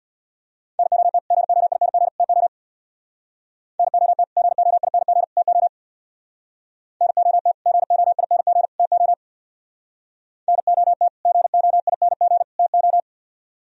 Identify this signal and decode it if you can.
Morse code, heard as a keyed tone, about 64 wpm, 700 Hz. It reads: NOT GOING TO NOT GOING TO NOT GOING TO NOT GOING TO